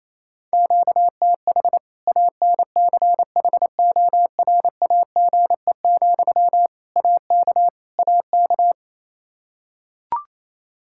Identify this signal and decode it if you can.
Morse code, heard as a keyed tone, about 28 wpm, 700 Hz. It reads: QTH ANCHORAGE, AK AK
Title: Practice QSO - Exchange 1 - Beginning - 28wpm